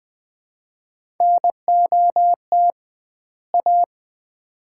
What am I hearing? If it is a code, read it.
Morse code, heard as a keyed tone, about 20 wpm, 700 Hz. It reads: NOT A
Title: 2x Word-Spacing - Sets of 2 Words - 20wpm